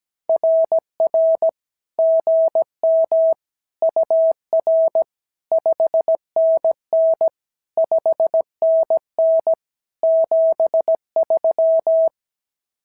Morse code, heard as a keyed tone, about 17 wpm, 650 Hz. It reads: RR GM UR 5NN 5NN 73
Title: SOTA QSO Practice 17wpm